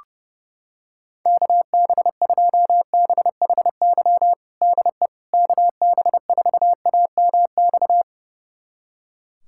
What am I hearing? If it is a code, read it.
Morse code, heard as a keyed tone, about 30 wpm, 700 Hz. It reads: KB2BHY DE KB4AMX
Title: USA Callsign DE Callsign - 6 Characters - 30wpm